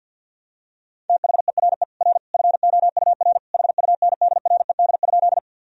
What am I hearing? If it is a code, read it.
Morse code, heard as a keyed tone, about 51 wpm, 700 Hz. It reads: THERE R FOUR HUNDRED?
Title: Sentences from Top 400 Words and Abbreviations 50wpm